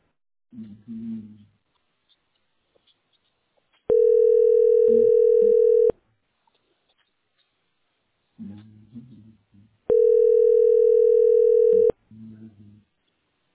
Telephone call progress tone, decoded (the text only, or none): ringback tone